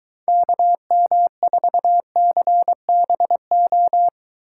Morse code, heard as a keyed tone, about 23 wpm, 700 Hz. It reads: KM4CBO